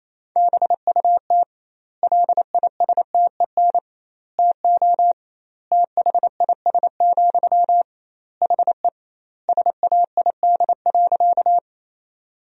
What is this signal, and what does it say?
Morse code, heard as a keyed tone, about 27 wpm, 700 Hz. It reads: BUT LISTEN TO THIS, HE SAID.
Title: 2x Word-Spacing - Sentences from Top 400 Words - 28wpm